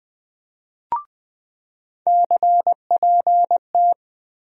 Morse code, heard as a keyed tone, about 20 wpm, 700 Hz. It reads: CPT
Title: USA Call Sign Postfixes - 3 Letters 20wpm